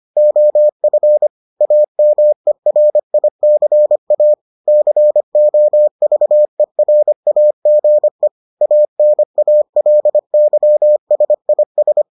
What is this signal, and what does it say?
Morse code, heard as a keyed tone, about 25 wpm, 600 Hz. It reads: OF AMERICA COVERAGE ANALYSIS